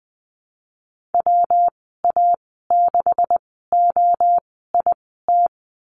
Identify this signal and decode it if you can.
Morse code, heard as a keyed tone, about 20 wpm, spaced as for 14 wpm, 700 Hz. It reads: WA6OIT
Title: 193 Lesson 31g - Letter v - Call Signs 20wpm